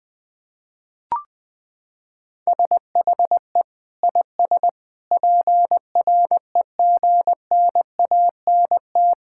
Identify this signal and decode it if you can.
Morse code, heard as a keyed tone, about 21 wpm, 700 Hz. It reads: SHE IS PREGNANT
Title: Sentences from Top 1000 Words - Review Part 6 - 20wpm